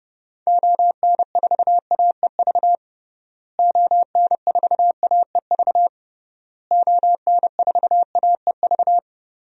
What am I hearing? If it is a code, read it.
Morse code, heard as a keyed tone, about 30 wpm, 700 Hz. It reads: ON4AEV ON4AEV ON4AEV
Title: DXCC-100 Call Signs - 6 Characters - Repeated 3x - 30wpm